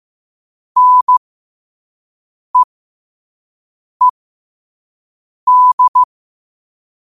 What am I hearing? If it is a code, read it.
Morse code, heard as a keyed tone, about 15 wpm, spaced as for 5 wpm, 1000 Hz. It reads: NEED
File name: qotdfiles / 5wpm